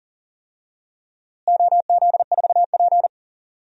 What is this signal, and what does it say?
Morse code, heard as a keyed tone, about 40 wpm, 700 Hz. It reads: OZ4P
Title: DXCC-100 Call Signs - 4 Characters - Repeated 2x - 40wpm